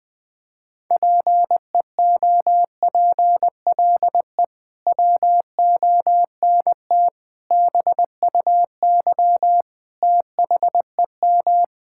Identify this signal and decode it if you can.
Morse code, heard as a keyed tone, about 20 wpm, 700 Hz. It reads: PEOPLE WONT BUY THEM
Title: Sentences from Top 1000 Words - Review Part 6 - 20wpm